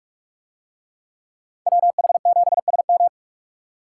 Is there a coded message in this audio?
Yes: Morse code from 1.7 to 3.1 s, heard as a keyed tone, about 45 wpm, 700 Hz. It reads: WH7SM